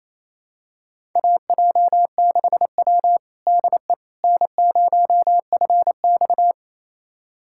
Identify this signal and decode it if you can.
Morse code, heard as a keyed tone, about 28 wpm, 690 Hz. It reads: AJ6W DE N0FX
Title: USA Callsign DE Callsign - 4 Characters - 28wpm